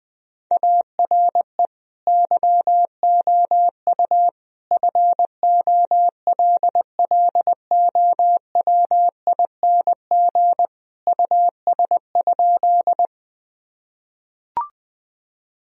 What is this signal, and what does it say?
Morse code, heard as a keyed tone, about 20 wpm, 700 Hz. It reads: ARE YOU FOLLOWING US?
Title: Sentences from Top 1000 Words - Review Part 4 - 20wpm